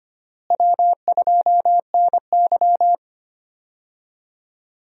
Morse code, heard as a keyed tone, about 25 wpm, 700 Hz. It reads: W2NY
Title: USA Call Signs - 4 Characters - Repeated 3x - 25wpm